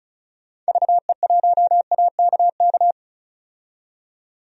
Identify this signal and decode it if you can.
Morse code, heard as a keyed tone, about 36 wpm, 700 Hz. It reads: VE1AKK
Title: DXCC-100 Call Signs - 6 Characters - 35wpm